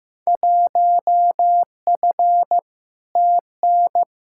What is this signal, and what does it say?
Morse code, heard as a keyed tone, about 15 wpm, 700 Hz. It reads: 1F TN
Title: Field Day Exchanges - 15wpm